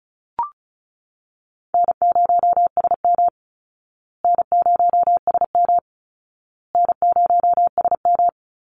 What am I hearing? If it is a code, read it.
Morse code, heard as a keyed tone, about 34 wpm, 700 Hz. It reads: N0SM N0SM N0SM
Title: Easy Call Signs Repeated 3x - 35wpm